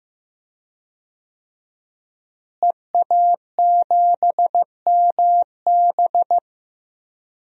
Morse code, heard as a keyed tone, about 16 wpm, 700 Hz. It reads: EA7MB
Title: DXCC-100 Call Signs - 5 Characters - Repeated 2x - 15wpm